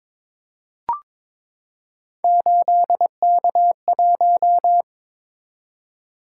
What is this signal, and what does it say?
Morse code, heard as a keyed tone, about 22 wpm, 700 Hz. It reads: 8K1
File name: practice / 22wpm / Three Letter-Number Combinations - 22wpm